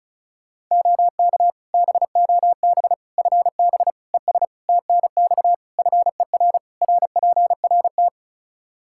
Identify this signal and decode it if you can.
Morse code, heard as a keyed tone, about 36 wpm, 700 Hz. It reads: OK BOB FB ES TNX FER RPRT
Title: Practice QSO - Exchange 2 - Info - 35wpm